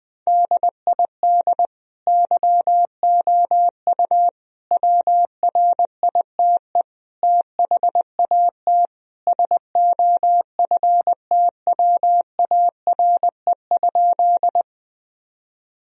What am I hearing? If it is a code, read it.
Morse code, heard as a keyed tone, about 20 wpm, 700 Hz. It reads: DID YOU WRITE THAT SOFTWARE?
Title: Sentences from Top 1000 Words - Review Part 3 - 20wpm